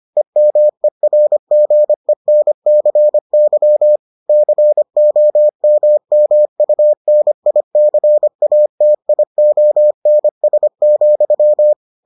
Morse code, heard as a keyed tone, about 25 wpm, 600 Hz. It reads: EMERGENCY COMMUNICATIONS,